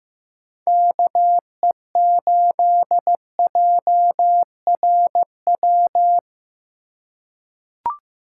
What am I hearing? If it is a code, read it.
Morse code, heard as a keyed tone, about 15 wpm, 700 Hz. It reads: KE8JRW
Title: USA Call Signs - 6 Characters - 2x Character Spacing - 15wpm